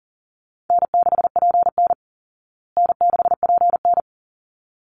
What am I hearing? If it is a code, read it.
Morse code, heard as a keyed tone, about 40 wpm, 700 Hz. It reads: N6PN N6PN
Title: Easy Call Signs Repeated 2x - 40wpm